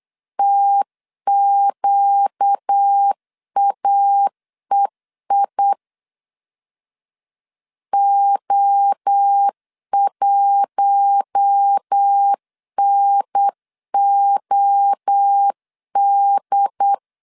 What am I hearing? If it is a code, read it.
Morse code, heard as a keyed tone, about 8 wpm, 800 Hz. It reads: TQAEI O1NOD
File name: CW / DET / 1 6 1500C S C 8wpm 5219